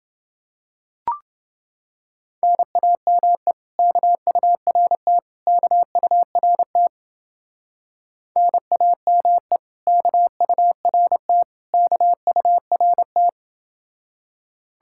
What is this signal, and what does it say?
Morse code, heard as a keyed tone, about 29 wpm, 700 Hz. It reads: NAME KURT KURT NAME KURT KURT